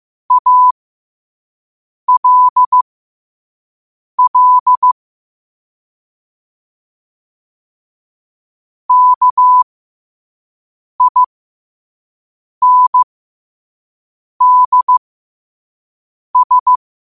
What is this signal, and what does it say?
Morse code, heard as a keyed tone, about 15 wpm, spaced as for 5 wpm, 1000 Hz. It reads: ALL KINDS
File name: qotdfiles / 5wpm